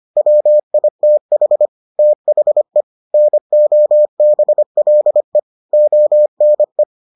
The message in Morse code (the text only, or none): WITH THE NOBLE ONE